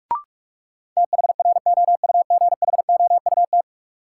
Morse code, heard as a keyed tone, about 46 wpm, 700 Hz. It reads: THROUGHOUT
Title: Sentences from Top 800 Words - New Word Drill - 45wpm